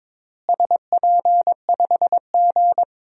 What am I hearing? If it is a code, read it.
Morse code, heard as a keyed tone, about 22 wpm, 700 Hz. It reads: SP5G
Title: DXCC-100 Call Signs - 4 Characters - Repeated 3x - 22wpm